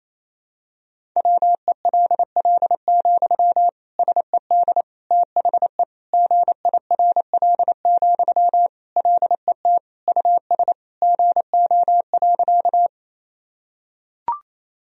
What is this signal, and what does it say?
Morse code, heard as a keyed tone, about 28 wpm, 700 Hz. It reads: WELL, SED THE GIRL, LET US GO.
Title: Sentences from Top 400 Words and Abbreviations 28wpm